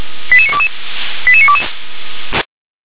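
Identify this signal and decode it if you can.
Five-tone selective call (ZVEI), heard as quick runs of five tones, tones of about 70 ms. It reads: 88922 88922
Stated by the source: Five-tone ZVEI selective call received on an SDR.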